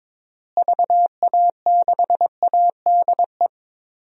Morse code, heard as a keyed tone, about 22 wpm, 700 Hz. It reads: VA6ADE
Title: DXCC-100 Call Signs - 6 Characters - Repeated 2x - 22wpm